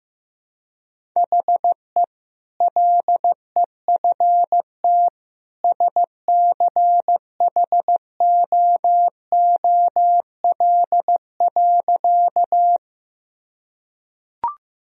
Morse code, heard as a keyed tone, about 16 wpm, 700 Hz. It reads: HE LEFT SCHOOL.
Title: Sentences from Top 1000 Words - Review Part 1 - 15wpm